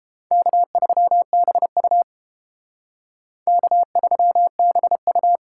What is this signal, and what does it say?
Morse code, heard as a keyed tone, about 31 wpm, 700 Hz. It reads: K3BU K3BU